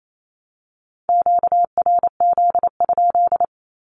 Morse code, heard as a keyed tone, about 28 wpm, 700 Hz. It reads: QRZ?
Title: Common QSO Phrases Encore 28wpm